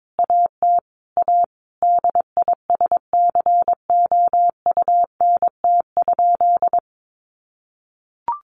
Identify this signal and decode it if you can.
Morse code, heard as a keyed tone, about 22 wpm, 700 Hz. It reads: AT A DISCOUNT?
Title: Sentences from Top 1200 Words 22wpm